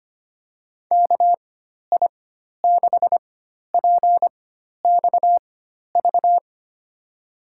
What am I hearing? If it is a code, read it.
Morse code, heard as a keyed tone, about 25 wpm, spaced as for 12 wpm, 700 Hz. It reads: KI6PXV
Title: USA Call Signs - 6 Characters - 4x Character Spacing - 25wpm